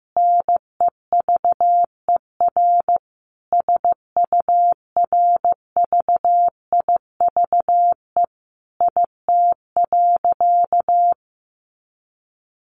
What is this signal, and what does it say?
Morse code, heard as a keyed tone, about 15 wpm, 700 Hz. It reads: NEVER SURVIVE IT.
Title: Sentences from Top 1300 Words 15wpm